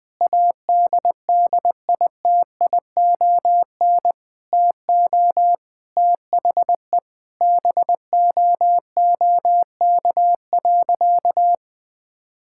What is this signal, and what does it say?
Morse code, heard as a keyed tone, about 21 wpm, 700 Hz. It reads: ADDITION TO THE BOOK.
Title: Sentences from Top 1000 Words - Review Part 6 - 20wpm